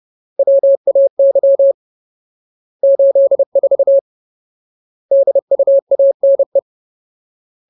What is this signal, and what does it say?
Morse code, heard as a keyed tone, about 30 wpm, 550 Hz. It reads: WAY 84 DUANE
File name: Ultimate Word List / 30WPM Ultimate